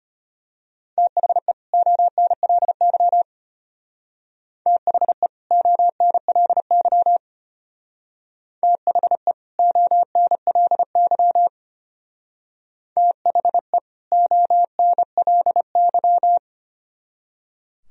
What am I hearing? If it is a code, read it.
Morse code, heard as a keyed tone, about 33 wpm, 700 Hz. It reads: THE ONLY THE ONLY THE ONLY THE ONLY